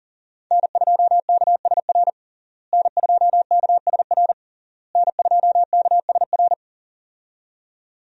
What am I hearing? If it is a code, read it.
Morse code, heard as a keyed tone, about 40 wpm, 700 Hz. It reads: N2KSR N2KSR N2KSR
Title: USA Call Signs - 5 Characters - Repeated 3x - 40wpm